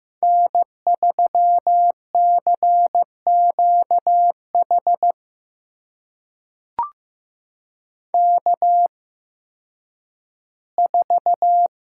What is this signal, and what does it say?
Morse code, heard as a keyed tone, about 15 wpm, spaced as for 10 wpm, 700 Hz. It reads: N3CQH K4
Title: USA Call Signs - 5 Characters - 8x Character Spacing - 15wpm